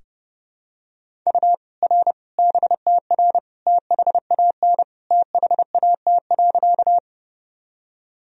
Morse code, heard as a keyed tone, about 30 wpm, 700 Hz. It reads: U R BTR THAN THAT.
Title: Sentences from Top 400 Words and Abbreviations 30wpm